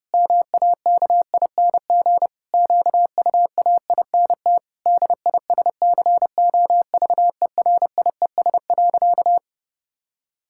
Morse code, heard as a keyed tone, about 30 wpm, 700 Hz. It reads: MAKING QUAINT DISCOVERIES.
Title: Pangram Challenge v2 - 30wpm